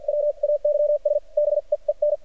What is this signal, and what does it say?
Morse code, heard as a keyed tone, about 50 wpm, spaced as for 40 wpm, 590 Hz. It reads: WAYS BEEN
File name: archived / CRSnet